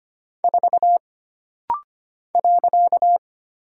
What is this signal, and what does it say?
Morse code, heard as a keyed tone, about 25 wpm, 700 Hz. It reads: 4 .
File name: Single Character (Letter:Number:Punctuation:Prosign) - Rapid-Fire - 25wpm